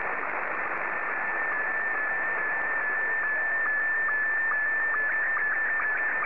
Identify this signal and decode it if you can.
SSTV (slow-scan television), Scottie 1.
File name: sstv / S1-VJ2OY